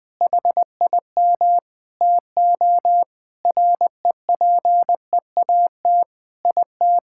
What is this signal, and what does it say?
Morse code, heard as a keyed tone, about 20 wpm, 700 Hz. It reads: HIM TO REPEAT IT
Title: Sentences from Top 1600 Words 20wpm